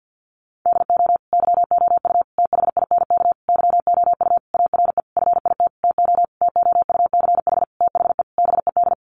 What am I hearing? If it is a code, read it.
Morse code, heard as a keyed tone, about 48 wpm, 700 Hz. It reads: DO YOU THINK YOU ARE FIT TO TOUCH THE BED
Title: Sentences from Top 1200 Words 50wpm